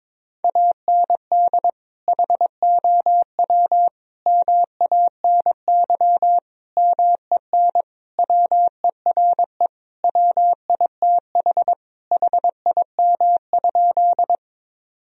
Morse code, heard as a keyed tone, about 22 wpm, 700 Hz. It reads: AND HOW MANY MEN WERE WITH HIM?
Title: Sentences from Top 1000 Words - Review Part 1 - 22wpm